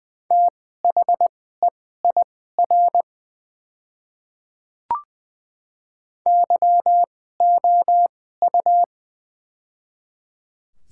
Morse code, heard as a keyed tone, about 20 wpm, spaced as for 15 wpm, 700 Hz. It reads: THEIR YOU
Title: 2x Character Spacing - Top 100 Words - 20wpm